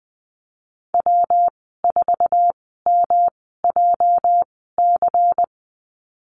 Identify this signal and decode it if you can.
Morse code, heard as a keyed tone, about 20 wpm, spaced as for 15 wpm, 700 Hz. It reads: W4MJC
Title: 219 Lesson 35g - Number 0 - Call Signs 20wpm